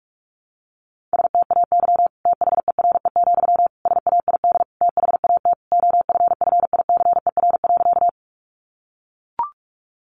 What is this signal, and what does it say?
Morse code, heard as a keyed tone, about 44 wpm, 700 Hz. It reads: STAY THERE, SAID THAT OFFICER.
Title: Sentences from Top 1200 Words 45wpm